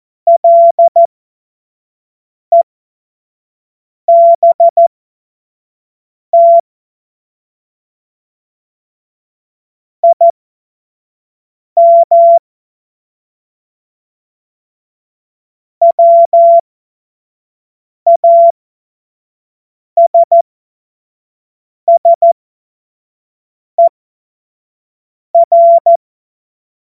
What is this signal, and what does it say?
Morse code, heard as a keyed tone, about 14 wpm, spaced as for 5 wpm, 680 Hz. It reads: LEBT IM WASSER
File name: Morsekurs / Witze / Witz 16. Februar 2021, 25 F70